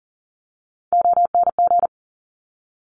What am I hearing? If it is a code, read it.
Morse code, heard as a keyed tone, about 42 wpm, 700 Hz. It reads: ONG
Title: Top Three Letter Combinations in Words 40wpm